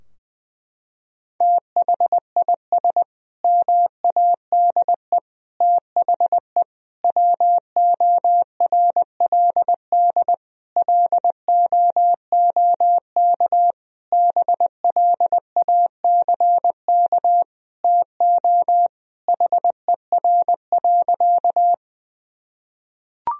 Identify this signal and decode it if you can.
Morse code, heard as a keyed tone, about 20 wpm, 700 Hz. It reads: THIS MADE THE WORLD LOOK BLACK TO HER.
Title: Sentences from Top 1000 Words - Review Part 2 - 20wpm